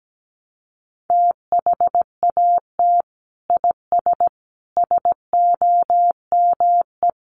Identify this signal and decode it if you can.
Morse code, heard as a keyed tone, about 17 wpm, 700 Hz. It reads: THAT IS SOME
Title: Sentences from Top 1200 Words 17wpm